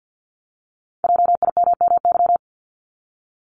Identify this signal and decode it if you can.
Morse code, heard as a keyed tone, about 50 wpm, 700 Hz. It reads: JIMMY
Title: Multi-letter Phonemes Repeated 3x 50wpm